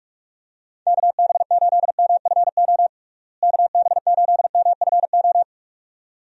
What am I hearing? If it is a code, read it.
Morse code, heard as a keyed tone, about 42 wpm, 700 Hz. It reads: KB8MFO KB8MFO
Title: USA Call Signs - 6 Characters - Repeated 2x - 45wpm